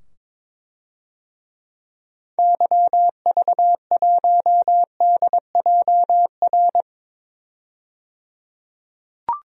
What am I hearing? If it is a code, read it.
Morse code, heard as a keyed tone, about 21 wpm, 700 Hz. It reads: YV1DJR
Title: DXCC-100 Call Signs - 6 Characters - 22wpm